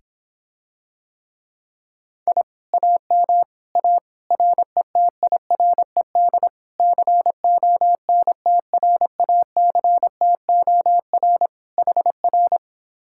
Morse code, heard as a keyed tone, about 26 wpm, 700 Hz. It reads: I AM A RETIRED CONTRACTOR HR